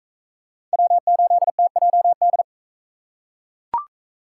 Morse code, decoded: W8TJD

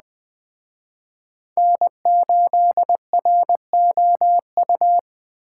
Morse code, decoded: N8ROU

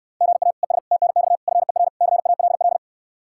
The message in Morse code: GA ES TNX FER CALL

